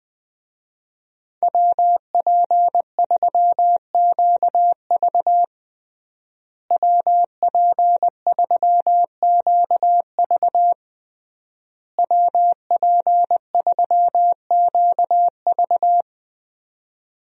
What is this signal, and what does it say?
Morse code, heard as a keyed tone, about 20 wpm, 700 Hz. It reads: WP3QV WP3QV WP3QV
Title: USA Call Signs - 5 Characters - Repeated 3x - 20wpm